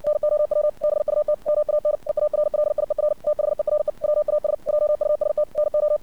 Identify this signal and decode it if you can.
Morse code, heard as a keyed tone, about 60 wpm, 600 Hz. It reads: NOW BUT MAN EARLIER THERE WAS JUST NO
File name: archived / CRSnet